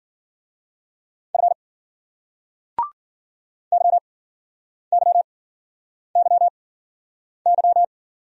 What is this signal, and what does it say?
Morse code, heard as a keyed tone, about 48 wpm, spaced as for 9 wpm, 700 Hz. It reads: F YYYY
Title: Speed-Racing Single Letters 40wpm